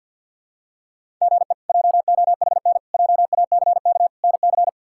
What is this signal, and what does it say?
Morse code, heard as a keyed tone, about 52 wpm, 700 Hz. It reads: GE JOHN JACK NC
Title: K1USN SST QSO Practice - 50wpm